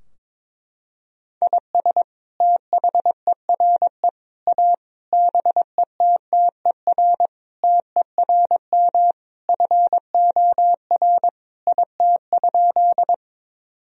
Morse code, heard as a keyed tone, about 22 wpm, 700 Hz. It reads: IS THERE A BETTER TERM FOR IT?
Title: Sentences from Top 1000 Words - Review Part 4 - 22wpm